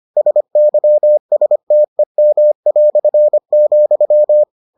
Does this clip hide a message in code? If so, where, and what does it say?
Yes: Morse code from 0.2 to 4.4 s, heard as a keyed tone, about 25 wpm, 600 Hz. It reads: SYSTEM",